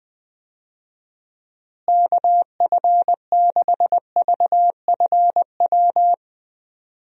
Morse code, heard as a keyed tone, about 20 wpm, 700 Hz. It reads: KF6VFW
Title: USA Call Signs - 6 Characters - Repeated 3x - 20wpm